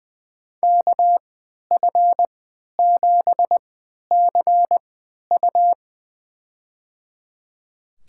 Morse code, heard as a keyed tone, about 20 wpm, spaced as for 11 wpm, 700 Hz. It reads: KF7CU
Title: USA Call Signs - 5 Characters - 3x Character Spacing - 20wpm